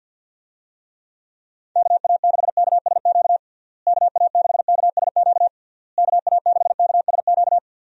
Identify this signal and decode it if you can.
Morse code, heard as a keyed tone, about 52 wpm, 700 Hz. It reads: KA6KSQ KA6KSQ KA6KSQ
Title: USA Call Signs - 6 Characters - Repeated 3x - 50wpm